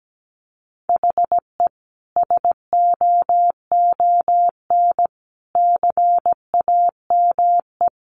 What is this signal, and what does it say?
Morse code, heard as a keyed tone, about 17 wpm, 700 Hz. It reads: HE SOON CAME
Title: Sentences from Top 600 Words 17wpm